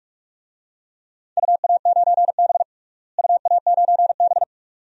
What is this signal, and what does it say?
Morse code, heard as a keyed tone, about 41 wpm, 700 Hz. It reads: UA9B UA9B
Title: DXCC-100 Call Signs - 4 Characters - Repeated 2x - 45wpm